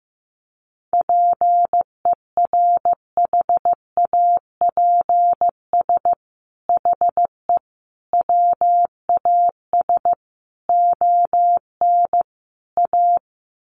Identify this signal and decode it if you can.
Morse code, heard as a keyed tone, about 15 wpm, 700 Hz. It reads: PERHAPS HE WAS ON A
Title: Sentences from Top 1100 Words 15wpm